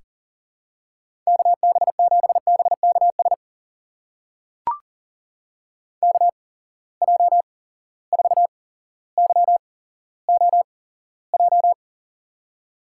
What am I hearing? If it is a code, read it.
Morse code, heard as a keyed tone, about 38 wpm, spaced as for 17 wpm, 700 Hz. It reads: KB7BKS KJ4YOJ
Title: USA Call Signs - 6 Characters - 8x Character Spacing - 40wpm